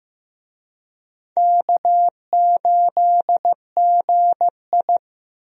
Morse code, heard as a keyed tone, about 15 wpm, 700 Hz. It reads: K8GI
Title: USA Call Signs - 4 Characters - 15wpm